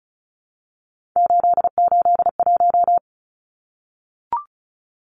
Morse code, heard as a keyed tone, about 34 wpm, 700 Hz. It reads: 881
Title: Numbers - 3-Digit 35wpm